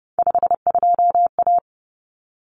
Morse code, heard as a keyed tone, about 30 wpm, 700 Hz. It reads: 52A
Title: Common Cut Numbers - 3-Digit 30wpm